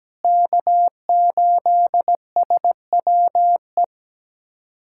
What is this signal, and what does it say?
Morse code, heard as a keyed tone, about 17 wpm, 700 Hz. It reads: K8SWE
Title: USA Callsign DE Callsign 17wpm